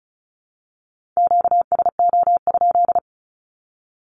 Morse code, heard as a keyed tone, about 34 wpm, 700 Hz. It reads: QSO?
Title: Q-Codes 35wpm